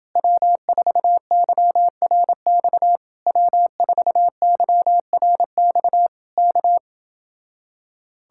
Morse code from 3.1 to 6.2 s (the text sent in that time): W4YRX